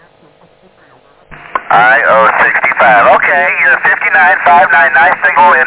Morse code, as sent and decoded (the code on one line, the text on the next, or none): none